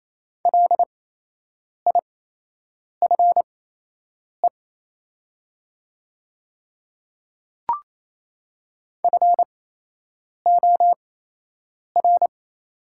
Morse code, heard as a keyed tone, about 28 wpm, spaced as for 8 wpm, 700 Hz. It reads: LIFE FOR
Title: 8x Character Spacing - Top 500 Words - 28wpm